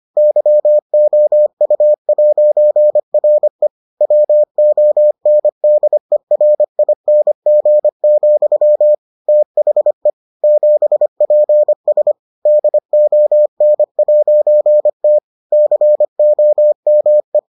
Morse code, heard as a keyed tone, about 25 wpm, 600 Hz. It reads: YOU'RE WONDERING, THE 7PS DON'T COME